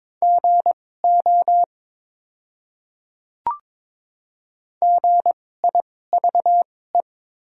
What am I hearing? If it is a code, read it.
Morse code, heard as a keyed tone, about 22 wpm, spaced as for 16 wpm, 700 Hz. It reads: GO GIVE
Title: 2x Character Spacing - Top 300 Words - 22wpm